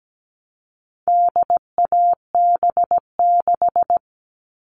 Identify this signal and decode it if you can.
Morse code, heard as a keyed tone, about 17 wpm, 700 Hz. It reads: DAB6
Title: All Cut Numbers - 4-Digit 17wpm